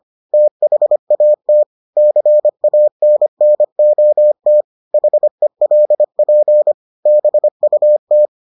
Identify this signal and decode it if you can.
Morse code, heard as a keyed tone, about 25 wpm, 600 Hz. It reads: THAT CANNOT HELP BUT